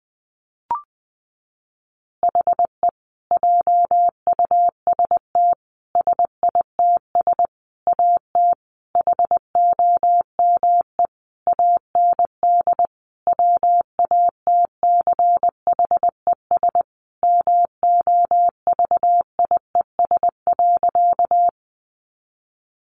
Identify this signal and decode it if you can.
Morse code, heard as a keyed tone, about 20 wpm, 700 Hz. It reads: HE JUST SITS AT HOME AND WATCHES MOVIES.
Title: Additional Easy Phrases 20wpm